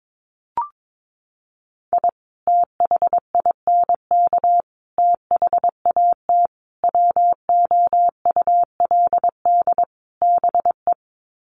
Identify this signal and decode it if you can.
Morse code, heard as a keyed tone, about 22 wpm, 700 Hz. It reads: I THINK THAT WOULD BE
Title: Sets of 5 Words Encore2 22wpm